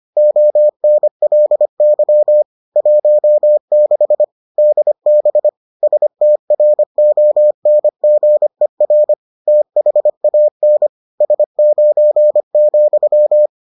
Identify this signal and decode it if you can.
Morse code, heard as a keyed tone, about 25 wpm, 600 Hz. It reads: ONLY 16 DB STRONGER THAN S9,